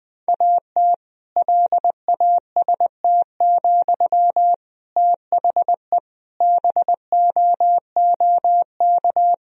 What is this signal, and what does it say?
Morse code, heard as a keyed tone, about 20 wpm, 700 Hz. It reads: AT LAST, THE BOOK
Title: Sentences from Top 1000 Words - Review Part 3 - 20wpm